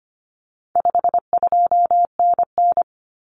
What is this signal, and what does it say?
Morse code, heard as a keyed tone, about 25 wpm, 700 Hz. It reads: 52NN
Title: Common Cut Numbers - 4-Digit 25wpm